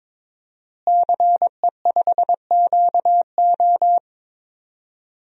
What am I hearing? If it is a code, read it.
Morse code, heard as a keyed tone, about 22 wpm, 700 Hz. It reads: CE5QO
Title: DXCC-100 Call Signs - 5 Characters - Repeated 2x - 22wpm